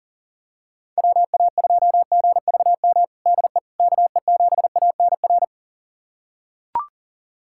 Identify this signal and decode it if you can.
Morse code, heard as a keyed tone, about 40 wpm, 690 Hz. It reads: WA2GVM DE KE7ANR
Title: USA Callsign DE Callsign - 6 Characters - 40wpm